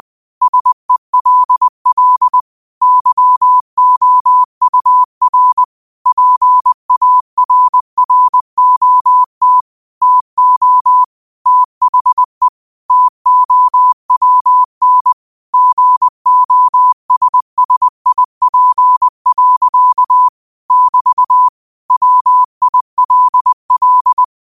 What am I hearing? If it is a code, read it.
Morse code, heard as a keyed tone, about 20 wpm, 1000 Hz. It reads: SELL YOUR PARROT TO THE TOWN GOSSIP. <BT> WILL
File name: qotdfiles / 20wpm